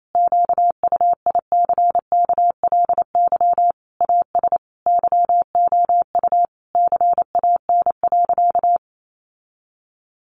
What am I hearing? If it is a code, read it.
Morse code, heard as a keyed tone, about 28 wpm, 700 Hz. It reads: QUICKLY AS YOU CAN.
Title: Sentences from Top 1100 Words 28wpm